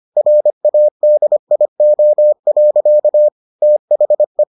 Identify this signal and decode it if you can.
Morse code, heard as a keyed tone, about 25 wpm, 600 Hz. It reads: RADIO. THE